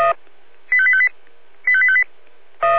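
Five-tone selective call (ZVEI), heard as quick runs of five tones, tones of about 70 ms. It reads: 85858 85858
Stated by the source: Five-tone ZVEI selective call received by SDR.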